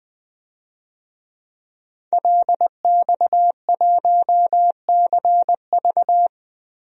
Morse code, heard as a keyed tone, about 20 wpm, 700 Hz. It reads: LX1CV
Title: DXCC-100 Call Signs - 5 Characters - 20wpm